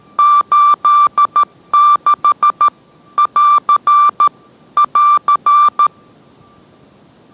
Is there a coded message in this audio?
Yes: Morse code from 0.2 to 5.9 s, heard as a keyed tone, about 13 wpm, 1170 Hz. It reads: 86 <AR> <AR>